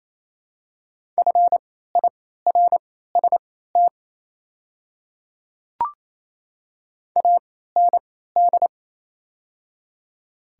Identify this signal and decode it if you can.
Morse code, heard as a keyed tone, about 28 wpm, spaced as for 16 wpm, 690 Hz. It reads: FIRST AND
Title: 3x Character Spacing - Top 100 Words - 28wpm